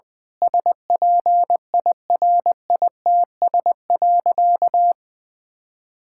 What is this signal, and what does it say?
Morse code, heard as a keyed tone, about 20 wpm, 700 Hz. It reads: SPIRITS.
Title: Difficult Sentences with Hints - 20wpm